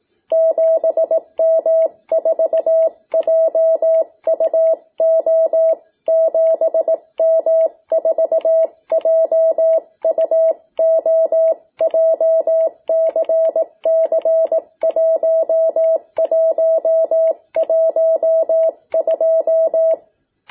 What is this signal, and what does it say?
Morse code, heard as a keyed tone, about 18 wpm, 660 Hz. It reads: 7M4JUO 7M4JUO JCC1112
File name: jh8cbh / 2021cw